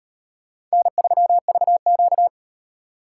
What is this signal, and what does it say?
Morse code, heard as a keyed tone, about 36 wpm, 700 Hz. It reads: N3VQ